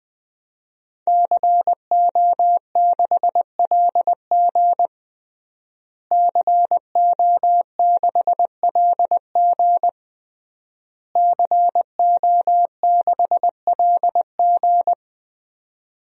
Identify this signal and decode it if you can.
Morse code, heard as a keyed tone, about 20 wpm, 700 Hz. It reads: CO6LG CO6LG CO6LG
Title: DXCC-100 Call Signs - 5 Characters - Repeated 3x - 20wpm